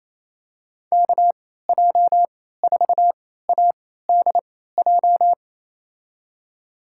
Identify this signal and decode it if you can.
Morse code, heard as a keyed tone, about 28 wpm, spaced as for 16 wpm, 700 Hz. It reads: KJ4ADJ
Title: USA Call Signs - 6 Characters - 3x Character Spacing - 28wpm